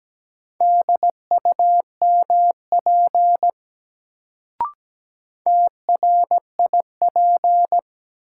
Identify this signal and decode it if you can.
Morse code, heard as a keyed tone, about 17 wpm, 700 Hz. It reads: DUMP TRIP
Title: Sentences from Top 1300 Words - New Word Drill Alt - 17wpm